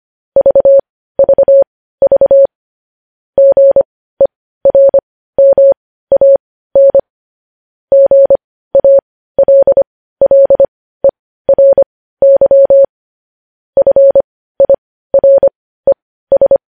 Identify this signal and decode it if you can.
Morse code, heard as a keyed tone, about 25 wpm, spaced as for 15 wpm, 550 Hz. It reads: VVV GERMAN GALLERY FIRES
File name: podcast / News headlines-1x-25x15